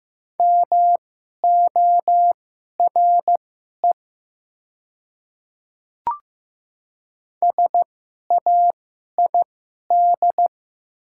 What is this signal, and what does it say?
Morse code, heard as a keyed tone, about 15 wpm, spaced as for 11 wpm, 700 Hz. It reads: MORE SAID
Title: 2x Character Spacing - Top 300 Words - 15wpm